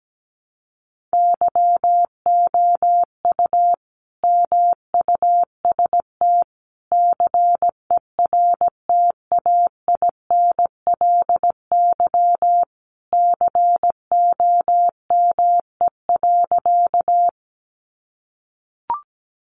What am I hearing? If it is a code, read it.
Morse code, heard as a keyed tone, about 17 wpm, 700 Hz. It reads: YOU MUST CERTAINLY COME.
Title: Sentences from Top 900 Words 17wpm